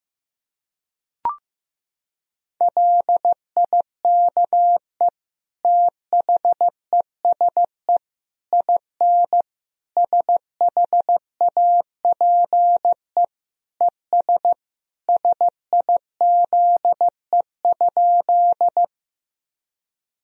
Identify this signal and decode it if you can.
Morse code, heard as a keyed tone, about 15 wpm, 700 Hz. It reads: LIKE THESE IN SHAPE ES SIZE?
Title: Sentences from Top 500 Words and Abbreviations 15wpm